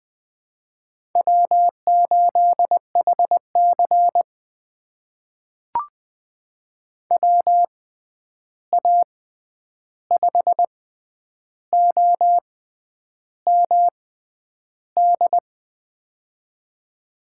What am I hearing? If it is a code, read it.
Morse code, heard as a keyed tone, about 20 wpm, spaced as for 9 wpm, 700 Hz. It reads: W8HC WA5OMD
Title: USA Call Signs - 6x Character Spacing - 20wpm